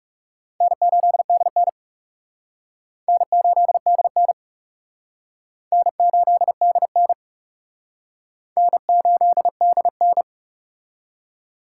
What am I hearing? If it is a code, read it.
Morse code, heard as a keyed tone, about 36 wpm, 700 Hz. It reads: N8DN N8DN N8DN N8DN
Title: Speed Racing - USA 1x2 Call Signs 30wpm